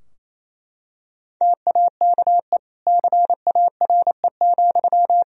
Morse code, heard as a keyed tone, about 27 wpm, 700 Hz. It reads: TAKE CARE,
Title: Sentences from Top 1000 Words - Review Part 4 - 28wpm